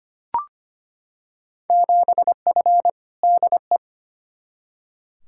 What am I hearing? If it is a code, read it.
Morse code, heard as a keyed tone, about 25 wpm, 700 Hz. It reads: 7F DE
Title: Field Day Exchanges Encore - 25wpm